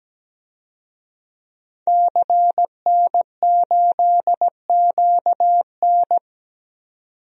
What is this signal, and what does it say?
Morse code, heard as a keyed tone, about 17 wpm, 700 Hz. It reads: CN8QN
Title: DXCC-100 Call Signs - 5 Characters - 17wpm